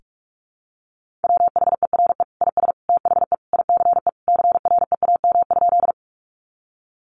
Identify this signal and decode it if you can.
Morse code, heard as a keyed tone, about 44 wpm, 700 Hz. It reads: WHERE IS THE ICE CREAM?
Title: Sentences from Top 1500 Words 45wpm